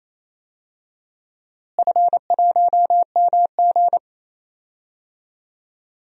Morse code, heard as a keyed tone, about 28 wpm, 700 Hz. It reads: F1MG
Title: DXCC-100 Call Signs - 4 Characters - Repeated 3x - 28wpm